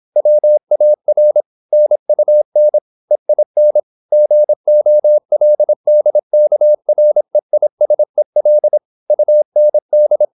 Morse code, heard as a keyed tone, about 26 wpm, 600 Hz. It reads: WAR NUN EIN GOLDKREISEL UND